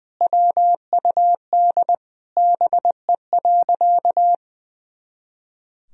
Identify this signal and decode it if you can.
Morse code, heard as a keyed tone, about 20 wpm, 700 Hz. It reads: WUD BE.
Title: Sentences from Top 200 Words and Abbreviations 20wpm